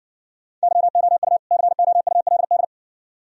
Morse code, heard as a keyed tone, about 59 wpm, 700 Hz. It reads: YOU COULD